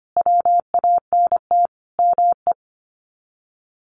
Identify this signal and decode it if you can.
Morse code, heard as a keyed tone, about 25 wpm, 700 Hz. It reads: WANT ME
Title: Sets of 5 Words Encore2 25wpm